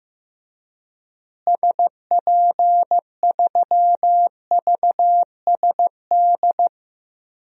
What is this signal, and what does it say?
Morse code, heard as a keyed tone, about 15 wpm, 700 Hz. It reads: SP3VSD
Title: DXCC-100 Call Signs - 6 Characters - Repeated 3x - 15wpm